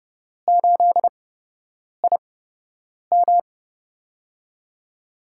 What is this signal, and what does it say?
Morse code, heard as a keyed tone, about 31 wpm, spaced as for 8 wpm, 700 Hz. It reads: 8IM
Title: USA Call Signs - 5 Characters - 8x Character Spacing - 30wpm